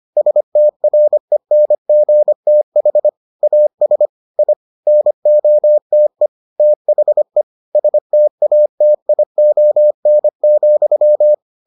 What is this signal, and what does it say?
Morse code, heard as a keyed tone, about 25 wpm, 600 Hz. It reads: STRENGTH AS I NOTE THE STATION,